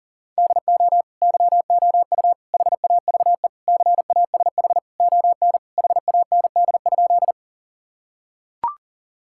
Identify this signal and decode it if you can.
Morse code, heard as a keyed tone, about 40 wpm, 700 Hz. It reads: DO YOU HAVE CASH ON HAND?
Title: Sentences from Top 1000 Words - Review Part 5 - 40wpm